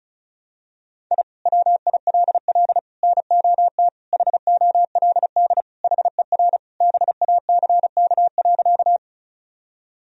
Morse code, heard as a keyed tone, about 36 wpm, 700 Hz. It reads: I WILL NOT HOLD HER BACK.
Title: Sentences from Top 1000 Words - Review Part 2 - 35wpm